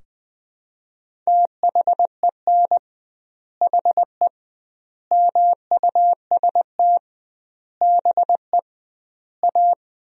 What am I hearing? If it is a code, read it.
Morse code, heard as a keyed tone, about 20 wpm, 700 Hz. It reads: THEN HE MUST BE A
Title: 2x Word-Spacing - Sentences from Top 500 Words - 20wpm